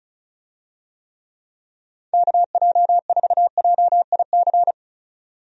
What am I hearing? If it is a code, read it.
Morse code, heard as a keyed tone, about 36 wpm, 700 Hz. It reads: KJ4JIC
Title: USA Call Signs - 6 Characters - 35wpm